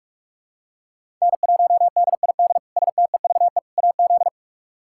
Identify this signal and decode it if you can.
Morse code, heard as a keyed tone, about 46 wpm, 700 Hz. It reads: N1DID STEVE AZ
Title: K1USN SST QSO Practice - 45wpm